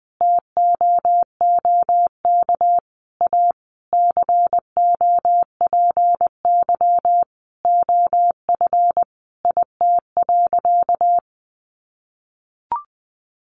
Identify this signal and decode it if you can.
Morse code, heard as a keyed tone, about 20 wpm, 700 Hz. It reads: TOOK A COPY OF IT.
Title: Sentences from Top 1400 Words 20wpm